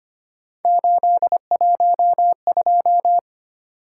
Morse code, heard as a keyed tone, about 25 wpm, 700 Hz. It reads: 812
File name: Three Letter-Number Combinations - 25wpm